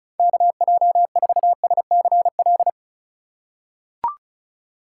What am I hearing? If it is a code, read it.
Morse code, heard as a keyed tone, about 34 wpm, 700 Hz. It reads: KJ4SCL
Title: USA Call Signs - 6 Characters - 3x Character Spacing - 35wpm